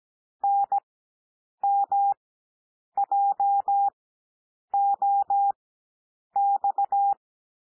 Morse code, heard as a keyed tone, about 17 wpm, spaced as for 8 wpm, 800 Hz. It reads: NMJOX
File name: cwmp3 / L2 SOUND LETTERS 17 wpm